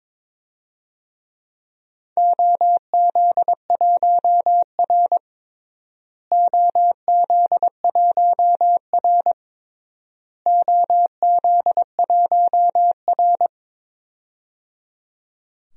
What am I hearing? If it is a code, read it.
Morse code, heard as a keyed tone, about 22 wpm, 700 Hz. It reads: OZ1R OZ1R OZ1R
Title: DXCC-100 Call Signs - 4 Characters - Repeated 3x - 22wpm